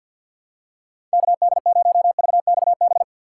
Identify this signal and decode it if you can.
Morse code, heard as a keyed tone, about 48 wpm, 700 Hz. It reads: KD0VXB
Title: USA Call Signs - 6 Characters - 6x Character Spacing - 50wpm